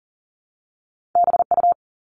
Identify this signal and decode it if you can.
Morse code, heard as a keyed tone, about 38 wpm, 700 Hz. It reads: BU
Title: All Cut Numbers - 2-Digit 40wpm